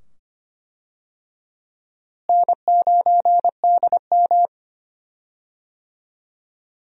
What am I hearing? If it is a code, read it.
Morse code, heard as a keyed tone, about 25 wpm, 700 Hz. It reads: N9DM
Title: USA Call Signs - 4 Characters - Repeated 2x - 25wpm